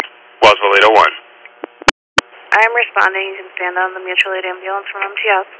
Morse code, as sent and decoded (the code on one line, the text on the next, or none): none